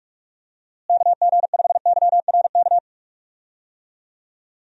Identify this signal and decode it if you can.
Morse code, heard as a keyed tone, about 46 wpm, 700 Hz. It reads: KG5YRK
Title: USA Call Signs - 6 Characters - Repeated 3x - 45wpm